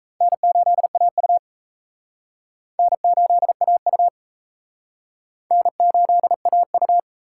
Morse code, heard as a keyed tone, about 36 wpm, 700 Hz. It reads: N8AU N8AU N8AU